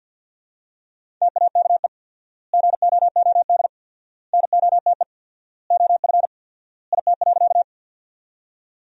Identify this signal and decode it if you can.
Morse code, heard as a keyed tone, about 50 wpm, 700 Hz. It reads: TAKE GOOD NOTE OF IT.
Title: 4x Word-Spacing - Sentences from Top 500 Words - 50wpm